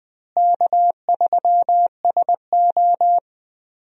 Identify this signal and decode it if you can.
Morse code, heard as a keyed tone, about 20 wpm, 700 Hz. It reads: K3SO
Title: USA Call Signs - 4 Characters - Repeated 3x - 20wpm